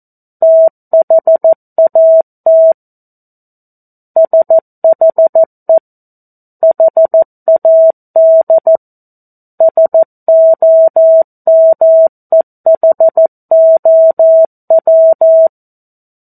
Morse code, heard as a keyed tone, about 14 wpm, 650 Hz. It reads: THAT SHE HAD SOMEHOW
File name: 02-Chapter-02 14 WPM